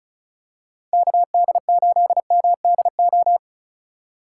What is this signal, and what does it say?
Morse code, heard as a keyed tone, about 36 wpm, 700 Hz. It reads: KD8MDO
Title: USA Call Signs - 6 Characters - 4x Character Spacing - 35wpm